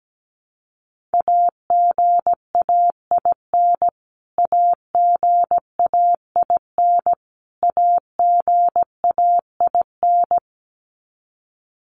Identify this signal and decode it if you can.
Morse code, heard as a keyed tone, about 17 wpm, 700 Hz. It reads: AGAIN AGAIN AGAIN
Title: Top 500 Words Repeated 17wpm